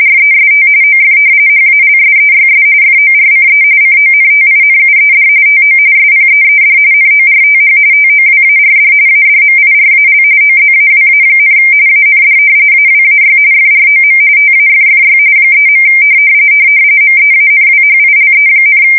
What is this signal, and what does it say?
RTTY, about 45.45 baud. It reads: REST OF HALSEYS FORCE CONTINUED TO SPEED NORTH TOWARD ADMIRAL OZAWA, AND AWAY FROM TAFFY 3. MEANWHILE,
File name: fsk170 / battleshipnj